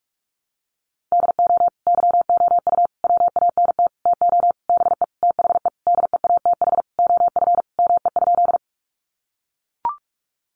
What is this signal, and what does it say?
Morse code, heard as a keyed tone, about 44 wpm, 700 Hz. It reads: DO YOU WANT TO BE THE DEATH OF ME?
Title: Sentences from Top 1100 Words 45wpm